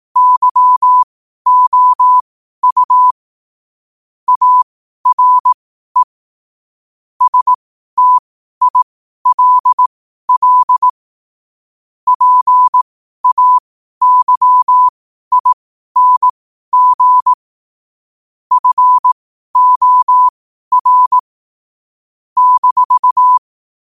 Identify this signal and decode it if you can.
Morse code, heard as a keyed tone, about 18 wpm, spaced as for 12 wpm, 1000 Hz. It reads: YOU ARE STILL PAYING FOR -
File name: qotdfiles / 12wpm